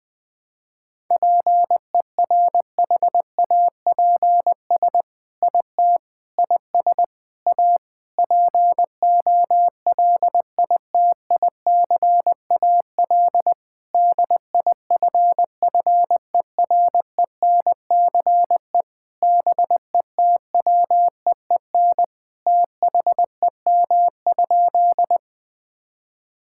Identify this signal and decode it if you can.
Morse code, heard as a keyed tone, about 20 wpm, 700 Hz. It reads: PERHAPS IT IS A POLITICAL DIFFERENCE BETWEEN THEM?
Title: Sentences from Top 1000 Words - Review Part 5 - 20wpm